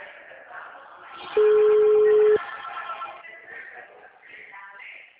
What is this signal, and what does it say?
Ringback tone (Europe, 425 Hz).
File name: out- -2045- - - .31604